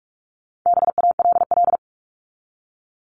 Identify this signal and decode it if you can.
Morse code, heard as a keyed tone, about 44 wpm, 700 Hz. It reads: BALL
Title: Top 751-1000 Words 45wpm